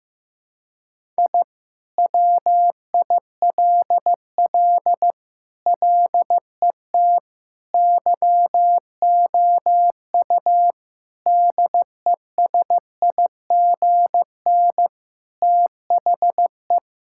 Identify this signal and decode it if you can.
Morse code, heard as a keyed tone, about 16 wpm, 700 Hz. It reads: I WILL LET YOU DESIGN THE
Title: Sentences from Top 1000 Words - Review Part 5 - 15wpm